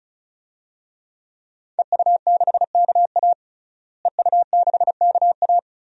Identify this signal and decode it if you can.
Morse code, heard as a keyed tone, about 34 wpm, 700 Hz. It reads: EU6KA EU6KA
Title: DXCC-100 Call Signs - 5 Characters - Repeated 2x - 35wpm